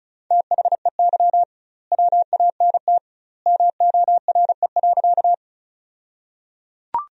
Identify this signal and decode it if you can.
Morse code, heard as a keyed tone, about 36 wpm, 700 Hz. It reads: THEY WANT MORE.
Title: 2x Word-Spacing - Sentences from Top 200 Words - 35wpm